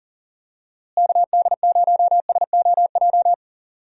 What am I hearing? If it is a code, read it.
Morse code, heard as a keyed tone, about 40 wpm, 700 Hz. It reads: KD0SOJ
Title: USA Call Signs - 6 Characters - Repeated 2x - 40wpm